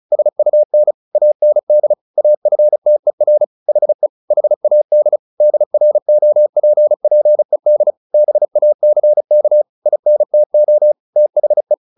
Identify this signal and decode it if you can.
Morse code, heard as a keyed tone, about 36 wpm, 600 Hz. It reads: SUN AND AFTER HE HAD DROPPED BACK INTO THE
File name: the old man and the sea / the old man and the sea 1 35wpm